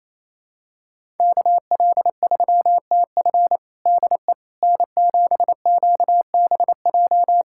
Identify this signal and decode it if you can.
Morse code, heard as a keyed tone, about 28 wpm, 700 Hz. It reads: KL3TF DE N7QBJ
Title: USA Callsign DE Callsign - 5 Characters - 28wpm